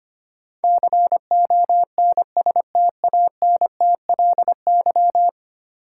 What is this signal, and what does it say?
Morse code, heard as a keyed tone, about 25 wpm, 700 Hz. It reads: CONSTANTLY
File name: 10 Letter Words - First 3 Letters Given - 25wpm